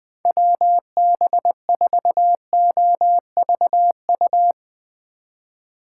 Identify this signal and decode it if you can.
Morse code, heard as a keyed tone, about 20 wpm, 700 Hz. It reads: WB4OVU